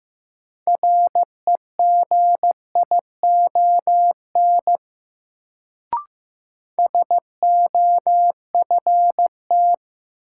Morse code, heard as a keyed tone, about 15 wpm, 700 Hz. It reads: REGION SOFT
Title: Sentences from Top 1500 Words - New Word Drill - 15wpm